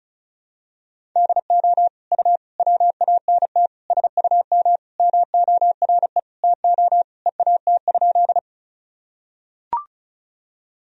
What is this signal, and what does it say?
Morse code, heard as a keyed tone, about 36 wpm, 700 Hz. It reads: DO U WANT SUM MORE TO EAT?
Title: Sentences from Top 300 Words and Abbreviations 35wpm